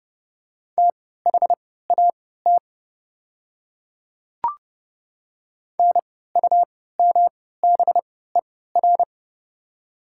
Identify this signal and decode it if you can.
Morse code, heard as a keyed tone, about 29 wpm, spaced as for 17 wpm, 700 Hz. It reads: THAT NUMBER
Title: 3x Character Spacing - Top 100 Words - 30wpm